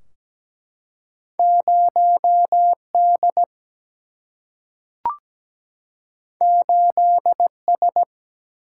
Morse code, heard as a keyed tone, about 17 wpm, 700 Hz. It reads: 0 D 8 S
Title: Single-Letter Single-Number 17wpm